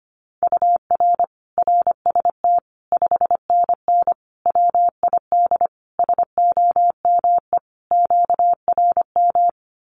Morse code, heard as a keyed tone, about 25 wpm, 700 Hz. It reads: UR RST 5NN WID SOME QRM
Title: Common QSO Phrases 25wpm